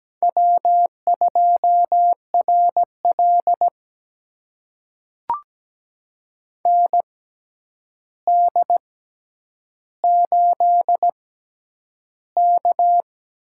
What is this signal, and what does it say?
Morse code, heard as a keyed tone, about 17 wpm, spaced as for 9 wpm, 700 Hz. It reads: W2RL ND8K
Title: USA Call Signs - 4 Characters - 6x Character Spacing - 17wpm